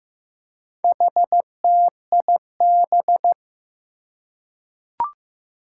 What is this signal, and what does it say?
Morse code, heard as a keyed tone, about 15 wpm, 700 Hz. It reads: HTIB
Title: Four Challenging Letter-Number Combinations 15wpm